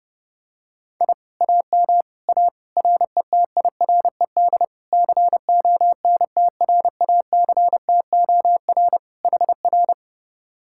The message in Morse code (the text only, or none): I AM A RETIRED CONTRACTOR HR